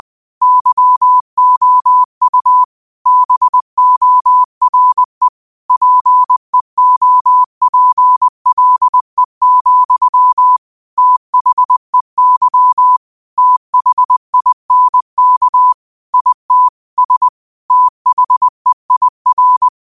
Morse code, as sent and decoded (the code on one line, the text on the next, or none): -.-- --- ..- / -... --- .-. / .--. --- .--. .-.. --..-- / - .... -.-- / - .... .. -. -.- / .. - / ... / - .... .. .-.
YOU BORE PEOPLE, THEY THINK IT S THEIR